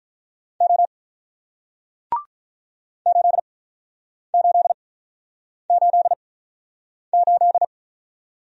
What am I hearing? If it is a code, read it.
Morse code, heard as a keyed tone, about 48 wpm, spaced as for 9 wpm, 700 Hz. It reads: O 8888